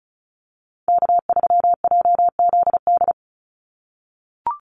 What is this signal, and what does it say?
Morse code, heard as a keyed tone, about 34 wpm, 700 Hz. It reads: K3JZD
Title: NA SOTA Chaser Callsigns 35wpm